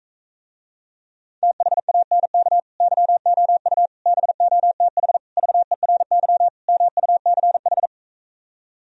Morse code, heard as a keyed tone, about 45 wpm, 700 Hz. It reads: THANK YOU BOTH VERY MUCH